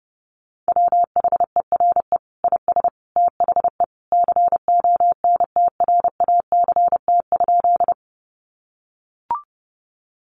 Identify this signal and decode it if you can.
Morse code, heard as a keyed tone, about 30 wpm, 700 Hz. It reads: WHERE IS THE CONTRACT?
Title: Sentences from Top 1400 Words 30wpm